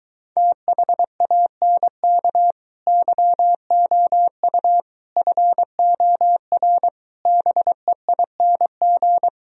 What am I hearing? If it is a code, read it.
Morse code, heard as a keyed tone, about 23 wpm, 690 Hz. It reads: THANK YOU FOR BEING